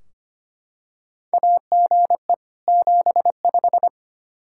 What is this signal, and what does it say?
Morse code, heard as a keyed tone, about 25 wpm, 700 Hz. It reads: AGE 75
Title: Practice QSO - Exchange 3 - More Info - 25wpm